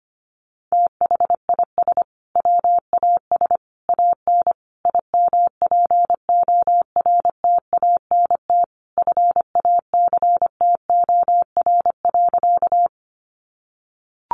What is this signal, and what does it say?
Morse code, heard as a keyed tone, about 25 wpm, 700 Hz. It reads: THIS WAS AN IMPORTANT FACTOR.
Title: Sentences from Top 1300 Words 25wpm